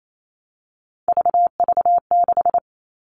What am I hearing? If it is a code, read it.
Morse code, heard as a keyed tone, about 28 wpm, 700 Hz. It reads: VV6
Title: All Cut Numbers - 3-Digit 28wpm